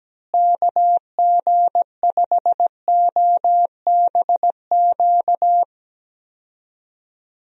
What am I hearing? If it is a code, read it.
Morse code, heard as a keyed tone, about 17 wpm, 700 Hz. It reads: KG5OBQ
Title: USA Call Signs - 6 Characters - Repeated 2x - 17wpm